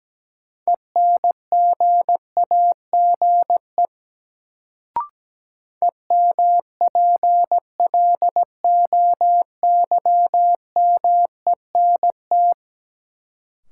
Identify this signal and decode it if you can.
Morse code, heard as a keyed tone, about 17 wpm, 700 Hz. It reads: ENGAGE EMPLOYMENT
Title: Sentences from Top 1500 Words - New Word Drill Alt - 17wpm